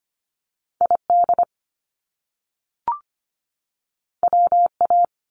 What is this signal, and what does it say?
Morse code, heard as a keyed tone, about 25 wpm, 700 Hz. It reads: ID WA
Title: US States 25wpm